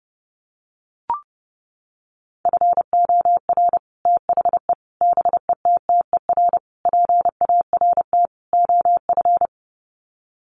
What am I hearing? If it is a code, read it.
Morse code, heard as a keyed tone, about 30 wpm, 700 Hz. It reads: FOR THE BETTER PART OF
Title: Sets of 5 Words Encore2 30wpm